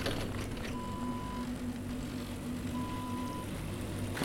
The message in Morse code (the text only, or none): none